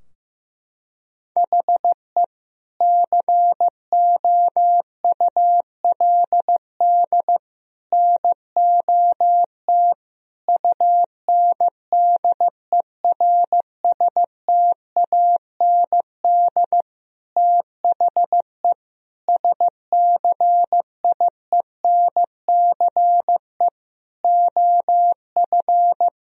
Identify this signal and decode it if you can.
Morse code, heard as a keyed tone, about 15 wpm, 700 Hz. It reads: HE COULD NOT UNDERSTAND THE SCIENCE OF
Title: Sentences from Top 1000 Words - Review Part 2 - 15wpm